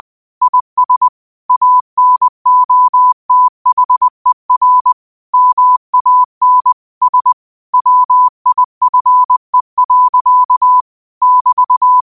Morse code, heard as a keyed tone, about 20 wpm, 1000 Hz. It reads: IS ANOTHER MAN S WIFE. <BT>
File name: qotdfiles / 20wpm